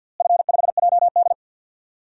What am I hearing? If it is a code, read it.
Morse code, heard as a keyed tone, about 47 wpm, 700 Hz. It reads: F5JD
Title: DXCC-100 Call Signs - 4 Characters - Repeated 3x - 50wpm